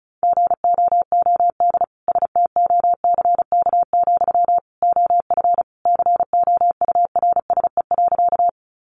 Morse code, heard as a keyed tone, about 34 wpm, 700 Hz. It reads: GOOD STOCK, OF COURSE.
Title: Sentences from Top 800 Words 35wpm